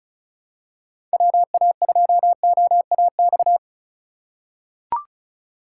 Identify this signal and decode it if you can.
Morse code, heard as a keyed tone, about 36 wpm, 700 Hz. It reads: WA2OAX
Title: USA Call Signs - 6 Characters - 3x Character Spacing - 35wpm